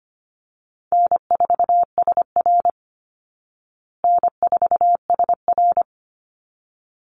Morse code, heard as a keyed tone, about 25 wpm, 700 Hz. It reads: N4SR N4SR
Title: Easy Call Signs Repeated 2x - 25wpm